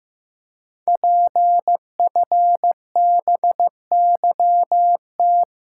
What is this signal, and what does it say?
Morse code, heard as a keyed tone, about 15 wpm, 700 Hz. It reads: PFBYT
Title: Code Groups - 5 Letters - 15wpm